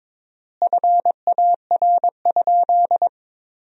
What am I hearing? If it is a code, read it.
Morse code, heard as a keyed tone, about 22 wpm, 700 Hz. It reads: FAR?
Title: 4x Word-Spacing - Sentences from Top 300 Words - 22wpm